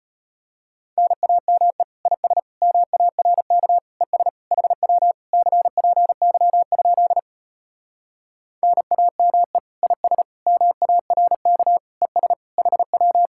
Morse code, heard as a keyed tone, about 33 wpm, 700 Hz. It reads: NAME IS MARK ES HW CPY? NAME IS MARK ES HW